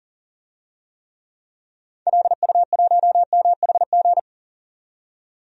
Morse code, heard as a keyed tone, about 40 wpm, 700 Hz. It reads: LU1MHG
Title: DXCC-100 Call Signs - 6 Characters - 40wpm